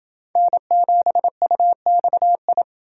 Morse code, heard as a keyed tone, about 29 wpm, 700 Hz. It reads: N7UXI